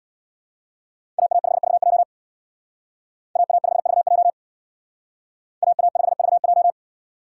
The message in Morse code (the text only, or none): AA5VJ AA5VJ AA5VJ